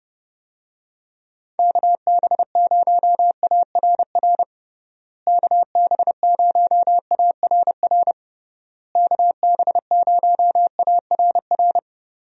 Morse code, heard as a keyed tone, about 32 wpm, 700 Hz. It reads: KB0ARR KB0ARR KB0ARR
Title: USA Call Signs - 6 Characters - Repeated 3x - 30wpm